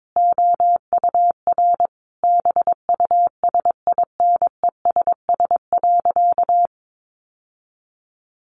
Morse code, heard as a keyed tone, about 22 wpm, 700 Hz. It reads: OUR BUSINESS.
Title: Sentences from Top 1400 Words 22wpm